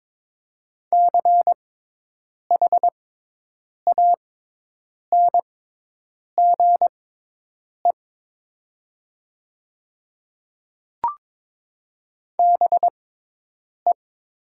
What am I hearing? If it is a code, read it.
Morse code, heard as a keyed tone, about 22 wpm, spaced as for 8 wpm, 700 Hz. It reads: CHANGE BE
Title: 6x Character Spacing - Top 200 Words - 22wpm